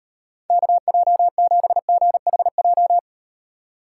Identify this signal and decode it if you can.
Morse code, heard as a keyed tone, about 38 wpm, 700 Hz. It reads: KJ7GHJ